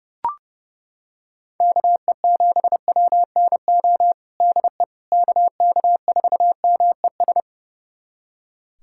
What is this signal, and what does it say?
Morse code, heard as a keyed tone, about 30 wpm, 700 Hz. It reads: KE7WNO DE KK4MES
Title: USA Callsign DE Callsign - 6 Characters - 30wpm